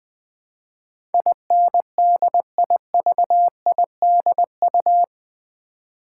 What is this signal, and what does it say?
Morse code, heard as a keyed tone, about 20 wpm, 700 Hz. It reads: INDIVIDU
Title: Build A Word - 4 Letters At A Time - 10-Letter Words - 20wpm